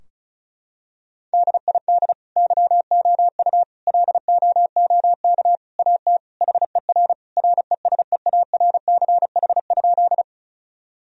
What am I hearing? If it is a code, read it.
Morse code, heard as a keyed tone, about 35 wpm, 700 Hz. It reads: DID YOU LOOK AT HER RESEARCH?
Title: Sentences from Top 1000 Words - Review Part 4 - 35wpm